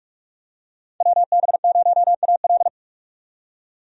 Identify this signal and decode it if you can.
Morse code, heard as a keyed tone, about 46 wpm, 700 Hz. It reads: WB0AL
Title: USA Call Signs - 5 Characters - 45wpm